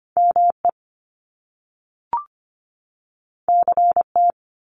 Morse code, heard as a keyed tone, about 25 wpm, 700 Hz. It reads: ME CT
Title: US States 25wpm